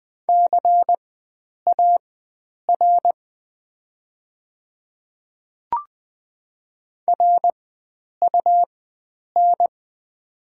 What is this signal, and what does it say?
Morse code, heard as a keyed tone, about 20 wpm, spaced as for 9 wpm, 690 Hz. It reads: CAR RUN
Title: 4x Character Spacing - Top 300 Words - 20wpm